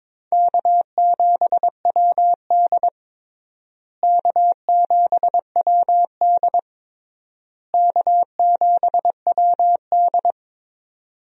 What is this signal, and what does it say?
Morse code, heard as a keyed tone, about 22 wpm, 700 Hz. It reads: K7WD K7WD K7WD
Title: USA Call Signs - 4 Characters - Repeated 3x - 22wpm